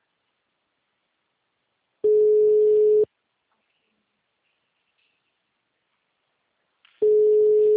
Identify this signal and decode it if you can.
Ringback tone (Europe, 425 Hz).
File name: out- -2011- - - .4